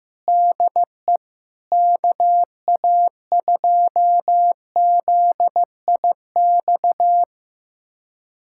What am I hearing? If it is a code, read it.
Morse code, heard as a keyed tone, about 16 wpm, 700 Hz. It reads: DE KA2ZIX
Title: USA Callsign DE Callsign 15wpm